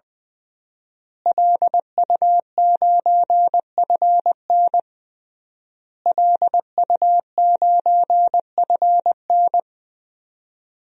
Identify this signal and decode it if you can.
Morse code, heard as a keyed tone, about 20 wpm, 700 Hz. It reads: LU9FN LU9FN
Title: DXCC-100 Call Signs - 5 Characters - Repeated 3x - 20wpm